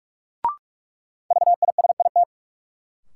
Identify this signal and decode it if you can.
Morse code, heard as a keyed tone, about 45 wpm, 700 Hz. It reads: VISIT
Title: Sentences from Top 900 Words - New Word Drill - 45wpm